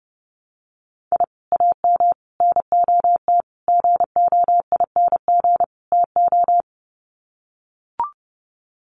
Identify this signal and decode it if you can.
Morse code, heard as a keyed tone, about 30 wpm, 700 Hz. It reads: I AM NOT GOING TO
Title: Sets of 5 Words Encore 30wpm